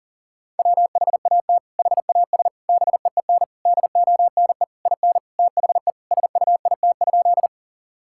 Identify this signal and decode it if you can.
Morse code, heard as a keyed tone, about 40 wpm, 700 Hz. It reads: WHAT HAS BEEN DONE IN THE SUIT?
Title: Sentences from Top 1600 Words 40wpm